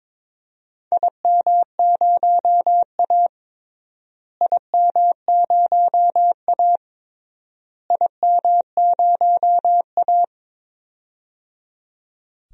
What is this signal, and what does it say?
Morse code, heard as a keyed tone, about 21 wpm, 700 Hz. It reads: IM0A IM0A IM0A
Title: DXCC-100 Call Signs - 4 Characters - Repeated 3x - 22wpm